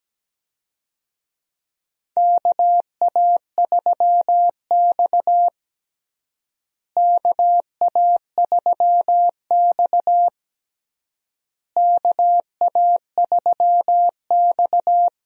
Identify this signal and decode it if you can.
Morse code, heard as a keyed tone, about 17 wpm, 700 Hz. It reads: KA3X KA3X KA3X
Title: USA Call Signs - 4 Characters - Repeated 3x - 17wpm